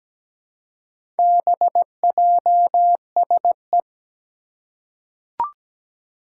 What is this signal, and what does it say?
Morse code, heard as a keyed tone, about 17 wpm, 700 Hz. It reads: BJSE
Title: Four Challenging Letter-Number Combinations 17wpm